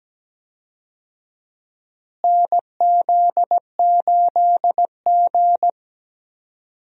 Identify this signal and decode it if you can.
Morse code, heard as a keyed tone, about 17 wpm, 700 Hz. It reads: NZ8G
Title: USA Call Signs - 4 Characters - Repeated 3x - 17wpm